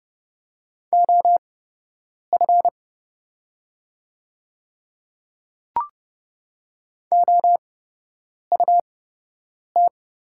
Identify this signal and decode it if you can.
Morse code, heard as a keyed tone, about 30 wpm, spaced as for 8 wpm, 700 Hz. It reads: OF OUT
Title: 8x Character Spacing - Top 100 Words - 30wpm